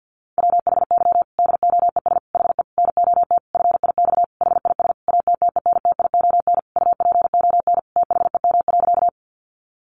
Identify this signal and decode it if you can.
Morse code, heard as a keyed tone, about 50 wpm, 700 Hz. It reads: WHY DOES HE NOT FIX HIS ATTENTION UPON THEM.
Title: Sentences from Top 1400 Words 50wpm